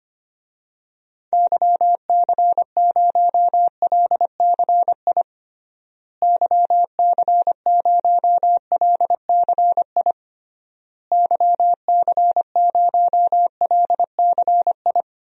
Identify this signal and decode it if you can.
Morse code, heard as a keyed tone, about 25 wpm, 700 Hz. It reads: YC0LCI YC0LCI YC0LCI
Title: DXCC-100 Call Signs - 6 Characters - Repeated 3x - 25wpm